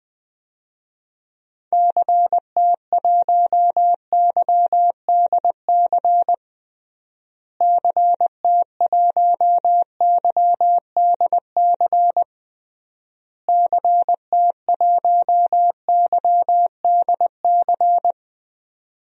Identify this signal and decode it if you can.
Morse code, heard as a keyed tone, about 20 wpm, 700 Hz. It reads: CT1YDC CT1YDC CT1YDC
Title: DXCC-100 Call Signs - 6 Characters - Repeated 3x - 20wpm